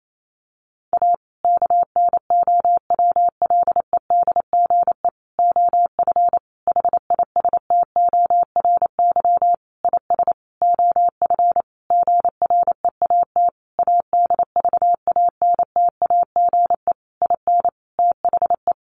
Morse code, heard as a keyed tone, about 28 wpm, 690 Hz. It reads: A KNOWLEDGE OF HISTORY IS OF GREAT ADVANTAGE IN THE
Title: Sentences from Top 900 Words 28wpm